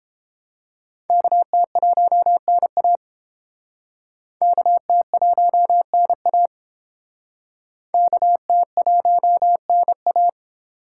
Morse code, heard as a keyed tone, about 30 wpm, 700 Hz. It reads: KT1NA KT1NA KT1NA